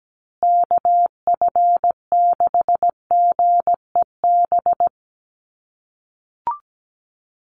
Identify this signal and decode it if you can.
Morse code, heard as a keyed tone, about 17 wpm, 700 Hz. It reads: KF6GEB
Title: USA Call Signs Repeated 3x 17wpm